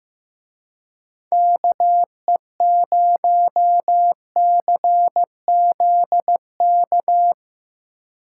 Morse code, heard as a keyed tone, about 16 wpm, 700 Hz. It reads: KE0CZK
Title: Speed Racing - USA 2x3 Call Signs 15wpm